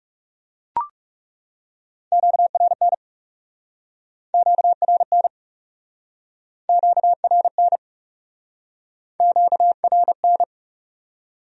Morse code, heard as a keyed tone, about 40 wpm, 700 Hz. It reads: QRN QRN QRN QRN